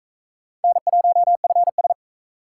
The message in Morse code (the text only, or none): N1FS